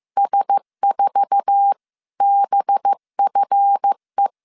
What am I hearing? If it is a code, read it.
Morse code, heard as a keyed tone, about 15 wpm, 800 Hz. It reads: S4BFE
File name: CW / DET / 1 9 1500C S C 14wpm 3245